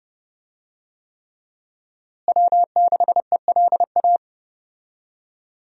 Morse code, heard as a keyed tone, about 28 wpm, 700 Hz. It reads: W6ELA
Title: USA Call Signs - 5 Characters - Repeated 3x - 30wpm